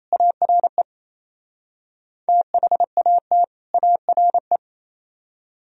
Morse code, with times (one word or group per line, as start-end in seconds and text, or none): 0.1-0.8 ARE
2.3-3.4 THAT
3.7-4.6 ARE